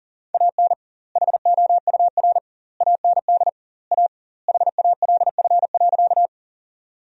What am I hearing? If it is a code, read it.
Morse code, heard as a keyed tone, about 40 wpm, 700 Hz. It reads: AN HOUR AND A HALF.
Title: 2x Word-Spacing - Sentences from Top 400 Words - 40wpm